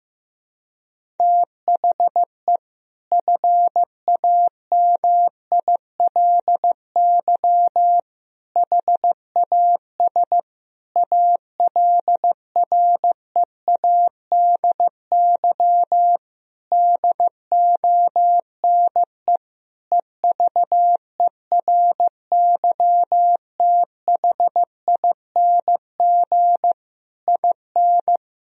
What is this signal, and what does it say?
Morse code, heard as a keyed tone, about 15 wpm, 700 Hz. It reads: THE FAMILY HAS ALREADY DONE EVERYTHING IN
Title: Sentences from Top 1000 Words - Review Part 2 - 15wpm